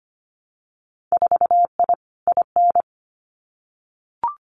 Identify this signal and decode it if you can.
Morse code, heard as a keyed tone, about 25 wpm, 700 Hz. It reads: 4I IN
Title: Winter Field Day Exchanges Encore 25wpm